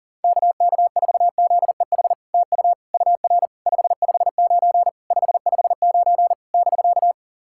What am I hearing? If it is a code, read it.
Morse code, heard as a keyed tone, about 40 wpm, 700 Hz. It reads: KK4ZEH TU UR 559 559 <BK>
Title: SOTA QSO Practice 40wpm